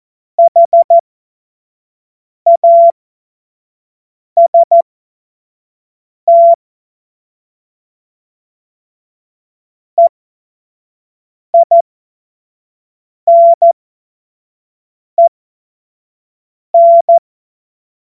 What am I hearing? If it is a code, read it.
Morse code, heard as a keyed tone, about 14 wpm, spaced as for 5 wpm, 680 Hz. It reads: HAST EINEN
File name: Morsekurs / Witze / Witz 12. April 2022 25 F70